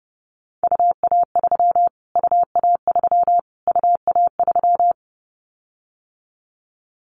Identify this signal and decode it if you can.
Morse code, heard as a keyed tone, about 30 wpm, 700 Hz. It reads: UA3 UA3 UA3
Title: Callsign Prefixes for Top 25 DXCC - Repeated 3x - 30wpm